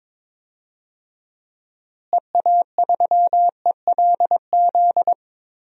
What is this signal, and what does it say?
Morse code, heard as a keyed tone, about 22 wpm, 700 Hz. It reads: EA3ELZ
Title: DXCC-100 Call Signs - 6 Characters - Repeated 2x - 22wpm